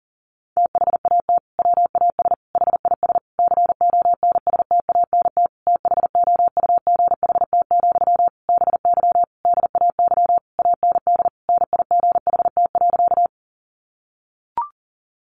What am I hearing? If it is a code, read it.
Morse code, heard as a keyed tone, about 40 wpm, 700 Hz. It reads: THAT WAS HIS CONSTANT THOUGHT, BY DAY AND NIGHT.
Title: Sentences from Top 1500 Words 40wpm